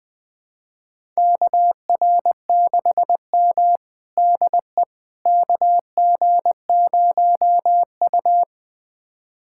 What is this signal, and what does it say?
Morse code, heard as a keyed tone, about 21 wpm, 700 Hz. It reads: KR6M DE KG0U
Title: USA Callsign DE Callsign - 4 Characters - 20wpm